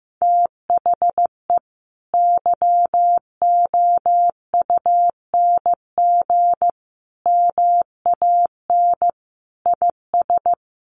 Morse code, heard as a keyed tone, about 15 wpm, 700 Hz. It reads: THE YOUNG MAN IS
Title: Sentences from Top 1300 Words 15wpm